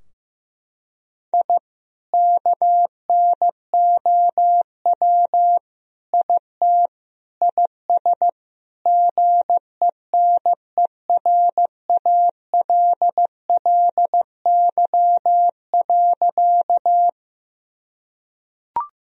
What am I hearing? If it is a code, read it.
Morse code, heard as a keyed tone, about 15 wpm, 700 Hz. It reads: I KNOW IT IS GENERALLY.
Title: Sentences from Top 1000 Words - Review Part 4 - 15wpm